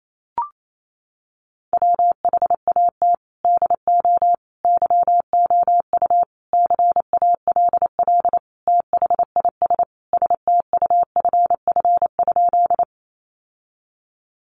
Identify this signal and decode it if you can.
Morse code, heard as a keyed tone, about 28 wpm, 700 Hz. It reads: WHAT DO YOU CALL THIS STUFF?
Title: Sentences from Top 1400 Words 28wpm